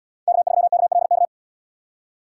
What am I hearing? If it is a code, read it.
Morse code, heard as a keyed tone, about 78 wpm, 700 Hz. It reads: K3KLK